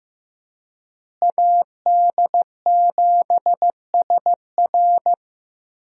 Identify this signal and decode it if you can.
Morse code, heard as a keyed tone, about 15 wpm, 700 Hz. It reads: AD7SR
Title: USA Call Signs - 5 Characters - Repeated 3x - 15wpm